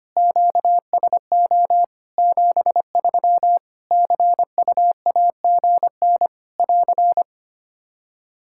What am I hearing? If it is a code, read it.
Morse code, heard as a keyed tone, about 25 wpm, 700 Hz. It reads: QSO 73 CUAGN <AR>
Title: Practice QSO - Exchange 4 - Ending - 25wpm